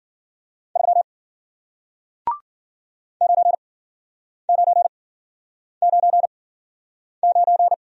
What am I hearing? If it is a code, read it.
Morse code, heard as a keyed tone, about 50 wpm, spaced as for 9 wpm, 700 Hz. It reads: 3 9999
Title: Speed-Racing Single Letter-Number 40wpm